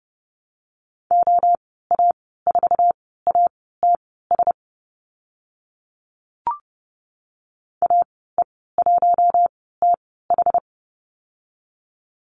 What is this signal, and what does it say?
Morse code, heard as a keyed tone, about 30 wpm, spaced as for 17 wpm, 700 Hz. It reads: OA4ATS AE1TH
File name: Lesson 15g - Number 5 - Call Signs 30wpm